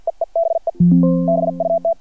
Morse code, heard as a keyed tone, about 50 wpm, 670 Hz. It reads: EE BE BUT
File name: archived / CRSnet